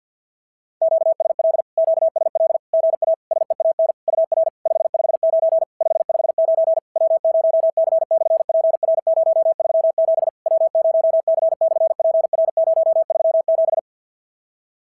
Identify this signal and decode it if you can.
Morse code, heard as a keyed tone, about 46 wpm, 660 Hz. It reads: QSL QSL GA SEAN UR 559 559 W0C/PR037 W0C/PR037
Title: SOTA QSO Practice 50wpm